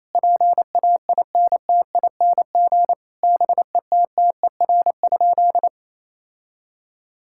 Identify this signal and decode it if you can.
Morse code, heard as a keyed tone, about 28 wpm, 700 Hz. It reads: PAINTING BETTER?
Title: Sentences from Top 1000 Words - Review Part 6 - 28wpm